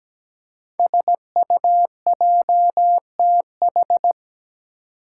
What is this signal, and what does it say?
Morse code, heard as a keyed tone, about 17 wpm, 700 Hz. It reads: SUJTH
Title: Code Groups - 5 Letter:Numbers - 17wpm